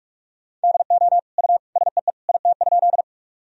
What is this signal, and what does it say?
Morse code, heard as a keyed tone, about 46 wpm, 700 Hz. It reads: DO U SEE IT?
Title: Sentences from Top 100 Words and Abbreviations 45wpm